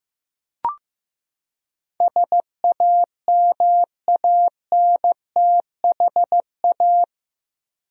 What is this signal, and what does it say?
Morse code, heard as a keyed tone, about 15 wpm, 700 Hz. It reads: SAMANTHA
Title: USA Common Names - 15wpm